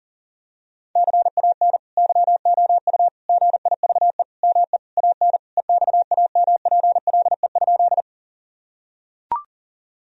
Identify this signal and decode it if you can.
Morse code, heard as a keyed tone, about 40 wpm, 700 Hz. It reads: CAN YOU GIVE ME AN EXAMPLE?
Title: Sentences from Top 1000 Words - Review Part 1 - 40wpm